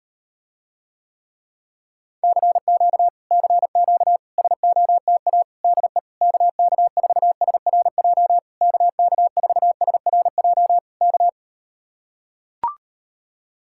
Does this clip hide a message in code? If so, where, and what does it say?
Yes: Morse code from 2.2 to 11.3 s, heard as a keyed tone, about 39 wpm, 700 Hz. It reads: CQ CQ SOTA DE KK4SRJ KK4SRJ K